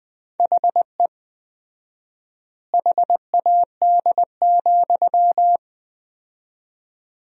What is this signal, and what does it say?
Morse code, heard as a keyed tone, about 20 wpm, 700 Hz. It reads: HE HAD,
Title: 4x Word-Spacing - Sentences from Top 500 Words - 20wpm